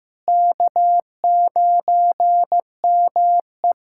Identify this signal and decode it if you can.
Morse code, heard as a keyed tone, about 15 wpm, 700 Hz. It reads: K9ME
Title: USA Callsign DE Callsign - 4 Characters - 15wpm